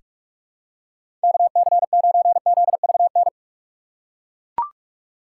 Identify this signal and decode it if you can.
Morse code, heard as a keyed tone, about 42 wpm, 700 Hz. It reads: KC9ZVN
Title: USA Call Signs - 6 Characters - 4x Character Spacing - 45wpm